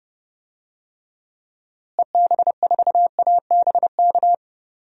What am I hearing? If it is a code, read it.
Morse code, heard as a keyed tone, about 30 wpm, 700 Hz. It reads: EB4ABK
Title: DXCC-100 Call Signs - 6 Characters - Repeated 3x - 30wpm